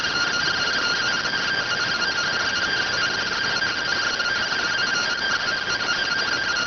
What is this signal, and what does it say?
Two-tone teleprinter signal, about 170 Hz shift.